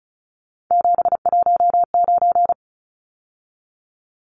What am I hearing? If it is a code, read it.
Morse code, heard as a keyed tone, about 34 wpm, 700 Hz. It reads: 719
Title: Numbers - 3-Digit 35wpm